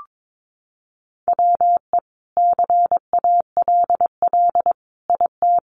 Morse code, heard as a keyed tone, about 22 wpm, 700 Hz. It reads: WE CALL IT
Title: Sentences from Top 400 Words 22wpm